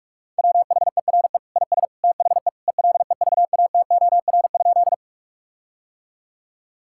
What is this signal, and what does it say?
Morse code, heard as a keyed tone, about 41 wpm, 700 Hz. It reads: WHERE IS THE ELEVATOR?
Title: Sentences from Top 1600 Words 45wpm